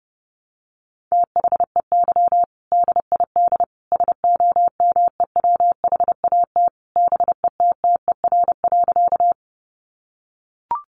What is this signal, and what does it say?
Morse code, heard as a keyed tone, about 30 wpm, 700 Hz. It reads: THEY DID SOMEWHAT BETTER.
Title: Sentences from Top 1300 Words 30wpm